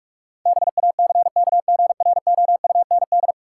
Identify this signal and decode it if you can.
Morse code, heard as a keyed tone, about 44 wpm, 700 Hz. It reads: BACKGROUND
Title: Sentences from Top 1300 Words - New Word Drill - 45wpm